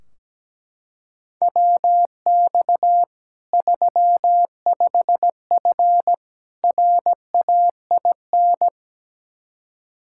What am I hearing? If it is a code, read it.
Morse code, heard as a keyed tone, about 17 wpm, 700 Hz. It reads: WX 35F RAIN
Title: K1USN SST QSO Practice 17wpm